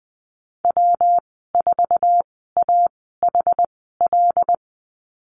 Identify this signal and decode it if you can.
Morse code, heard as a keyed tone, about 20 wpm, spaced as for 14 wpm, 700 Hz. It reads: W4AHL
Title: 122 Lesson 20g - Number 3 - Call Signs 20wpm